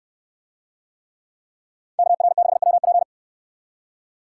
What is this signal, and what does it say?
Morse code, heard as a keyed tone, about 64 wpm, 700 Hz. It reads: KR6WP